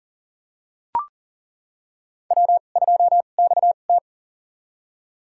Morse code, heard as a keyed tone, about 39 wpm, spaced as for 29 wpm, 690 Hz. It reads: W2XT
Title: USA Call Signs - 2x Character Spacing - 40wpm